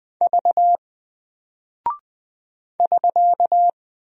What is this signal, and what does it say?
Morse code, heard as a keyed tone, about 20 wpm, 700 Hz. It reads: V <SK>
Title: Single Character (Letter:Number:Punctuation:Prosign) Alt 20wpm